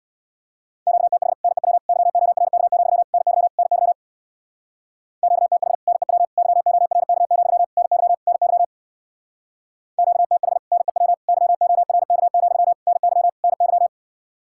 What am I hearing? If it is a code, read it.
Morse code, heard as a keyed tone, about 66 wpm, 700 Hz. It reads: QTH NEW YORK, NY NY QTH NEW YORK, NY NY QTH NEW YORK, NY NY